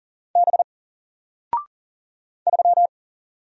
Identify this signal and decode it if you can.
Morse code, heard as a keyed tone, about 40 wpm, 700 Hz. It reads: B 3
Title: Single Letter-Number 40wpm